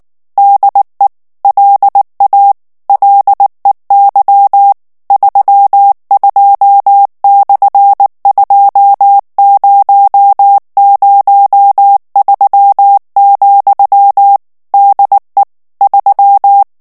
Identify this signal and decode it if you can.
Morse code, heard as a keyed tone, about 19 wpm, 800 Hz. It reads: DE LA LEY 32/2003, DE 3
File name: morse mp3 / ESPAGNOL TEXTE EN CLAIR 2 VITESSE 18 MOTS MINUTES